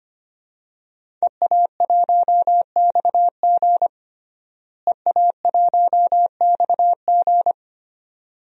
Morse code, heard as a keyed tone, about 25 wpm, 700 Hz. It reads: EA1XG EA1XG
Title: DXCC-100 Call Signs - 5 Characters - Repeated 2x - 25wpm